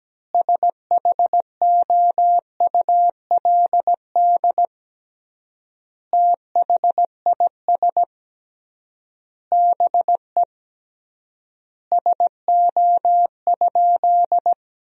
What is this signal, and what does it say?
Morse code, heard as a keyed tone, about 17 wpm, 700 Hz. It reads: SHOULD THIS BE SO?
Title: 3x Word-Spacing - Sentences from Top 300 Words - 17wpm